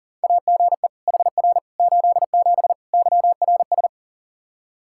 Morse code, heard as a keyed tone, about 37 wpm, 700 Hz. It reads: AGE HR 87 YRS
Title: Practice QSO - Exchange 3 - More Info - 40wpm